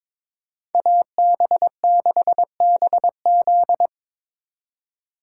Morse code, heard as a keyed tone, about 26 wpm, 700 Hz. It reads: AB6BZ
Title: Speed Racing - USA 2x2 Call Signs 22wpm